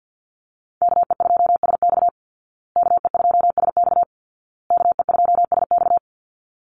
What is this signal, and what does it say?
Morse code, heard as a keyed tone, about 50 wpm, 700 Hz. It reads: KE2SX KE2SX KE2SX
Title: Call Signs Repeated 2x - 50wpm